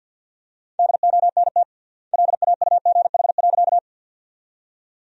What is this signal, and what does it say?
Morse code, heard as a keyed tone, about 51 wpm, 700 Hz. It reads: DONT LAUGH.
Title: 3x Word-Spacing - Sentences from Top 500 Words - 50wpm